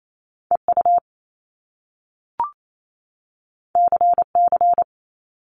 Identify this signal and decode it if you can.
Morse code, heard as a keyed tone, about 28 wpm, 700 Hz. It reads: EU CC
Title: Top Two Letter Combinations in Words 28wpm